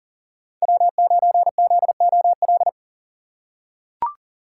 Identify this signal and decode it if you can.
Morse code, heard as a keyed tone, about 40 wpm, 700 Hz. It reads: W9ZOL
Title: USA Call Signs - 5 Characters - 2x Character Spacing - 40wpm